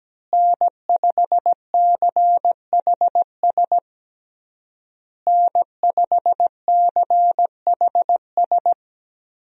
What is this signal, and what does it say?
Morse code, heard as a keyed tone, about 17 wpm, 700 Hz. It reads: N5CHS N5CHS
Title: USA Call Signs - 5 Characters - Repeated 3x - 17wpm